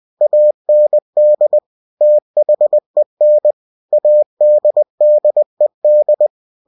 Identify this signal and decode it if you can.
Morse code, heard as a keyed tone, about 20 wpm, 600 Hz. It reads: AND THEN ADDED
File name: cw-books / winnie-the-pooh / winnie-20wpm-page-0036